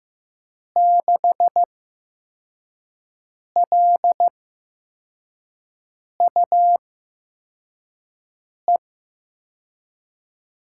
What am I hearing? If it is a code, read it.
Morse code, heard as a keyed tone, about 15 wpm, spaced as for 4 wpm, 700 Hz. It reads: 6LUE
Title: USA Call Signs - 6 Characters - 8x Character Spacing - 15wpm